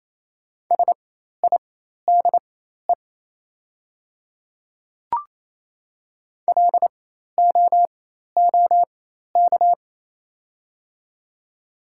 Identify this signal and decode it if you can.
Morse code, heard as a keyed tone, about 28 wpm, spaced as for 13 wpm, 700 Hz. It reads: SIDE LOOK
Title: 4x Character Spacing - Top 100 Words - 28wpm